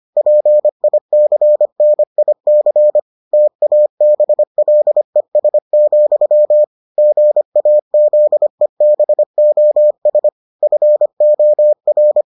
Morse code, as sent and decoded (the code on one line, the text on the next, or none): .--. .. -.-. -. .. -.-. / - .- -... .-.. ... --..-- / --. .- --.. -... --- ... / ..-. --- .-.
PICNIC TABLES, GAZEBOS FOR